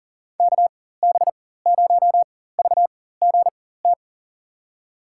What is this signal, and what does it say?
Morse code, heard as a keyed tone, about 40 wpm, spaced as for 19 wpm, 700 Hz. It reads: KB0VGT
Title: USA Call Signs - 4x Character Spacing - 40wpm